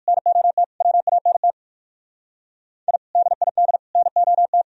Morse code, heard as a keyed tone, about 56 wpm, 700 Hz. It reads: NOT WANT I DID NOT
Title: Speed-Racing - Sets of 4 Words 45wpm